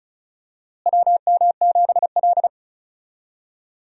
Morse code, heard as a keyed tone, about 36 wpm, 700 Hz. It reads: WM7L
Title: USA Call Signs - 4 Characters - Repeated 3x - 35wpm